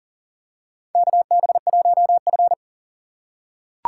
Morse code, heard as a keyed tone, about 38 wpm, 700 Hz. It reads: KB1F
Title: USA Call Signs - 4 Characters - 4x Character Spacing - 40wpm